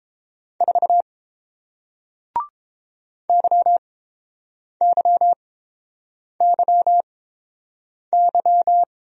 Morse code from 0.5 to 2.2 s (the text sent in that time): 4